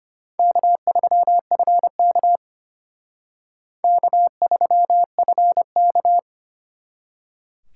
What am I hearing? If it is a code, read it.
Morse code, heard as a keyed tone, about 33 wpm, 700 Hz. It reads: K3FK K3FK